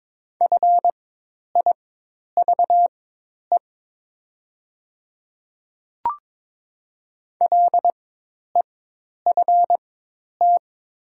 Morse code, heard as a keyed tone, about 22 wpm, spaced as for 10 wpm, 700 Hz. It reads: FIVE LEFT
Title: 4x Character Spacing - Top 500 Words - 22wpm